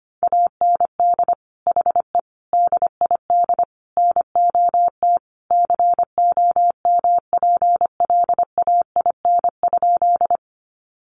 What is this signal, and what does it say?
Morse code, heard as a keyed tone, about 25 wpm, 700 Hz. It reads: AND HE DID NOT COMPLAIN?
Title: Sentences from Top 1300 Words 25wpm